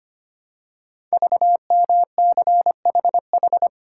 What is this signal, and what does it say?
Morse code, heard as a keyed tone, about 25 wpm, 700 Hz. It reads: VMCHH
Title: Code Groups - 5 Letter:Numbers - 25wpm